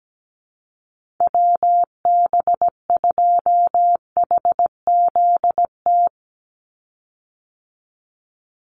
Morse code, heard as a keyed tone, about 17 wpm, 700 Hz. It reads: WB2HZT
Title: USA Call Signs 17wpm